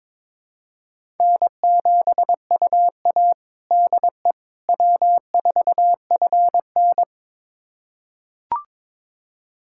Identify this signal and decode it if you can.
Morse code, heard as a keyed tone, about 22 wpm, 700 Hz. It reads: N7UA DE W4FN
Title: USA Callsign DE Callsign - 4 Characters - 22wpm